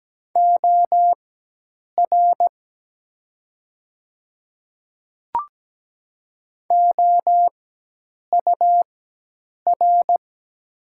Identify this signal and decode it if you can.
Morse code, heard as a keyed tone, about 17 wpm, spaced as for 8 wpm, 700 Hz. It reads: OR OUR
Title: 4x Character Spacing - Top 200 Words - 17wpm